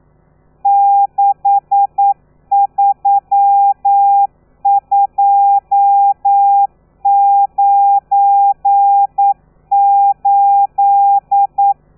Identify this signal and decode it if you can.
Morse code, heard as a keyed tone, about 9 wpm, 800 Hz. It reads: 63298
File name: telegrafia / i2tfj / Lez12 40